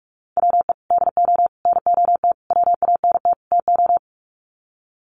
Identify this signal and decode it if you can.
Morse code, heard as a keyed tone, about 44 wpm, 700 Hz. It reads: WE DO NOT WANT TO
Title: Sets of 5 Words 45wpm